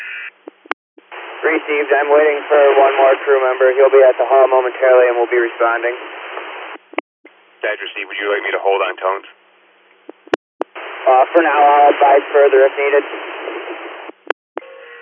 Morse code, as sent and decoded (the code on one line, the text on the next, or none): none